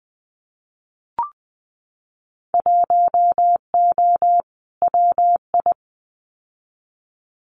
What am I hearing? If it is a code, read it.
Morse code, heard as a keyed tone, about 20 wpm, 700 Hz. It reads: 1O WI
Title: Winter Field Day Exchanges Encore 20wpm